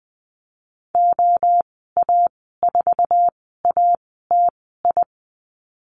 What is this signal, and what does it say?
Morse code, heard as a keyed tone, about 20 wpm, spaced as for 15 wpm, 700 Hz. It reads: OA4ATI
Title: 048 Lesson 9g - Number 4 - Call Signs 20wpm